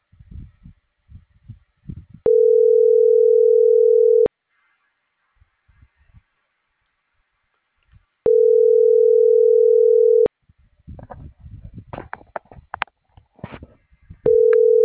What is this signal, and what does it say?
Ringback tone (North America), 440 + 480 Hz.